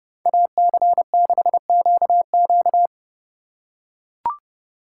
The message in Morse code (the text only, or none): AC6QQ